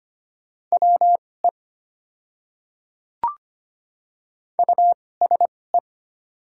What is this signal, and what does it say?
Morse code, heard as a keyed tone, about 25 wpm, spaced as for 18 wpm, 700 Hz. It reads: WE USE
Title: 2x Character Spacing - Top 100 Words - 25wpm